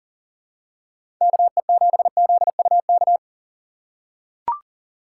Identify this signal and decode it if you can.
Morse code, heard as a keyed tone, about 39 wpm, 700 Hz. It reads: KE7ZUK
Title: USA Call Signs - 6 Characters - 6x Character Spacing - 40wpm